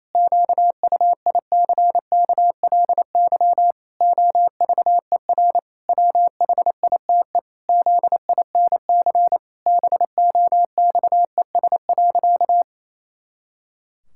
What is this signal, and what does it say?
Morse code, heard as a keyed tone, about 28 wpm, 700 Hz. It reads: QUICKLY OVER WHITE ZINC BOXES.
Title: Pangram Challenge v2 - 28wpm